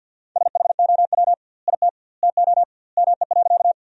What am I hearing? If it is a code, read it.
Morse code, heard as a keyed tone, about 52 wpm, 700 Hz. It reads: SHOW IT TO ME.
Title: 2x Word-Spacing - Sentences from Top 200 Words - 50wpm